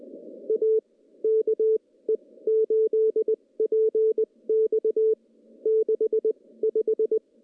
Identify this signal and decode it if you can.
Morse code, heard as a keyed tone, about 19 wpm, spaced as for 16 wpm, 440 Hz. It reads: A KE8PX 65
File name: contests / CW-SS-k5tr- -0100